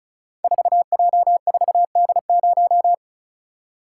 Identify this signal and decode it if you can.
Morse code, heard as a keyed tone, about 36 wpm, 700 Hz. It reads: 4J4D0
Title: Code Groups - 5 Letter:Numbers - 35wpm